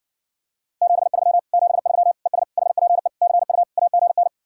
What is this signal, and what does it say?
Morse code, heard as a keyed tone, about 66 wpm, 700 Hz. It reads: 73 73 ES HPE CU AGN